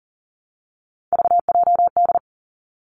Morse code, heard as a keyed tone, about 40 wpm, 700 Hz. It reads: VJD
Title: Three Challenging Letter-Number Combinations 40wpm